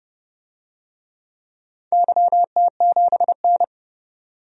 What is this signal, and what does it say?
Morse code, heard as a keyed tone, about 30 wpm, 700 Hz. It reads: YT7N
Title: DXCC-100 Call Signs - 4 Characters - 30wpm